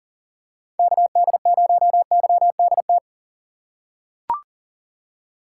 Morse code, heard as a keyed tone, about 40 wpm, 700 Hz. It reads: KD0YDT
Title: USA Call Signs - 6 Characters - 3x Character Spacing - 40wpm